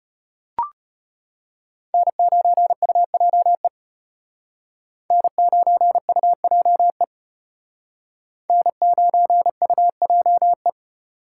Morse code, heard as a keyed tone, about 32 wpm, 700 Hz. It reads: N9UJE N9UJE N9UJE